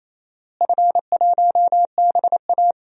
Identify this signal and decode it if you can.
Morse code, heard as a keyed tone, about 28 wpm, 700 Hz. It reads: F1BA
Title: DXCC-100 Call Signs - 4 Characters - Repeated 2x - 28wpm